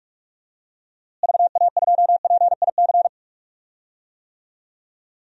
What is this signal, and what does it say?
Morse code, heard as a keyed tone, about 46 wpm, 700 Hz. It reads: VA2PIC
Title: DXCC-100 Call Signs - 6 Characters - Repeated 3x - 45wpm